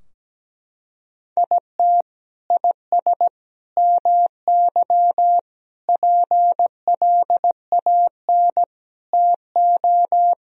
Morse code, heard as a keyed tone, about 17 wpm, 700 Hz. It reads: IT IS MY PLAN TO
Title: Sentences from Top 1000 Words - Review Part 2 - 17wpm